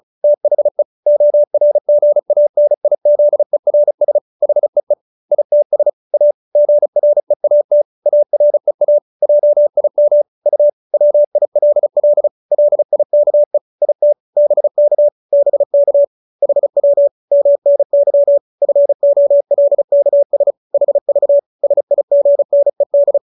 Morse code, heard as a keyed tone, about 34 wpm, 590 Hz. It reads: THE ORGANIZERS HEE ITS A GREAT AREA JIM U WILL LIKE IT BK BK HW MNY FOLKS HV SIGNED
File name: QSO312 35